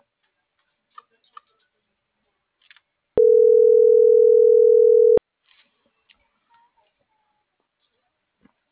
Ringback tone (North America), 440 + 480 Hz.
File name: CENTRAL NMP / out jacqueline.castronmp